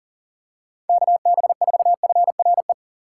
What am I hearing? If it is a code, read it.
Morse code, heard as a keyed tone, about 40 wpm, 700 Hz. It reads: KB4FRE
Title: USA Call Signs - 6 Characters - 8x Character Spacing - 40wpm